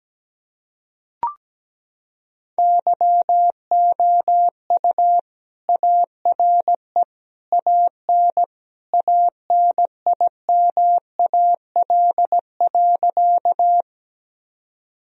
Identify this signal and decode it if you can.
Morse code, heard as a keyed tone, about 17 wpm, 700 Hz. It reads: YOU ARE AN ANIMAL.
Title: Sentences from Top 1000 Words - Review Part 1 - 17wpm